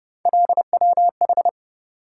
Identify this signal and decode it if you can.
Morse code, heard as a keyed tone, about 28 wpm, 700 Hz. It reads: LWH